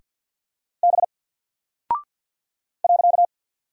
Morse code, heard as a keyed tone, about 48 wpm, 700 Hz. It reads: B .
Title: Single Character (Letter:Number:Punctuation:Prosign) 50wpm